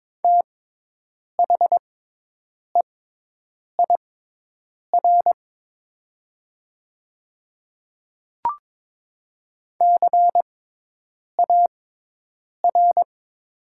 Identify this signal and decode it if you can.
Morse code, heard as a keyed tone, about 22 wpm, spaced as for 8 wpm, 700 Hz. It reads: THEIR CAR
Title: 6x Character Spacing - Top 500 Words - 22wpm